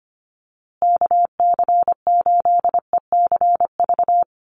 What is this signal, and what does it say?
Morse code, heard as a keyed tone, about 25 wpm, 700 Hz. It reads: KC8ECV
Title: USA Call Signs Repeated 2x 25wpm